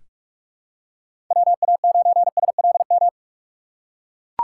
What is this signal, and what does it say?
Morse code, heard as a keyed tone, about 43 wpm, 700 Hz. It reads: WA9SLM
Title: USA Call Signs - 6 Characters - 8x Character Spacing - 45wpm